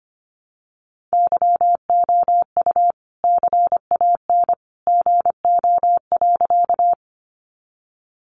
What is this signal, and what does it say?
Morse code, heard as a keyed tone, about 25 wpm, 700 Hz. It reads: YOU CAN GO.
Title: Sentences from Top 100 Words 25wpm